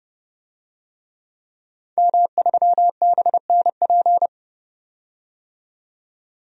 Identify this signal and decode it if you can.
Morse code, heard as a keyed tone, about 32 wpm, 700 Hz. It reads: M3BNP
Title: DXCC-100 Call Signs - 5 Characters - 30wpm